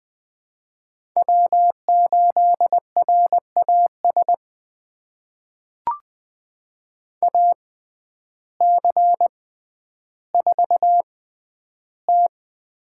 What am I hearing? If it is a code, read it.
Morse code, heard as a keyed tone, about 21 wpm, spaced as for 11 wpm, 700 Hz. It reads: W8RAS AC4T
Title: USA Call Signs - 5 Characters - 6x Character Spacing - 20wpm